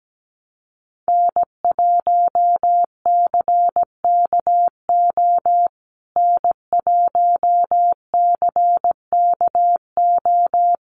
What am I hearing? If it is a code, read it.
Morse code, heard as a keyed tone, about 17 wpm, 700 Hz. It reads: N1CKO N1CKO
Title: USA Call Signs Repeated 3x 17wpm